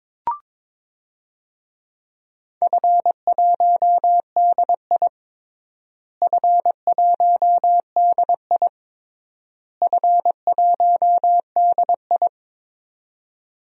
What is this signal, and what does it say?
Morse code, heard as a keyed tone, about 22 wpm, 700 Hz. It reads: F1DI F1DI F1DI
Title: DXCC-100 Call Signs - 4 Characters - Repeated 3x - 22wpm